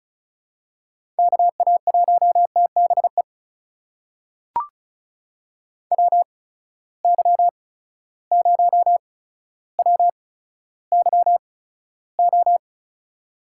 Morse code, heard as a keyed tone, about 36 wpm, spaced as for 15 wpm, 700 Hz. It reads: KA1TBE WY0WYO
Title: USA Call Signs - 6 Characters - 8x Character Spacing - 35wpm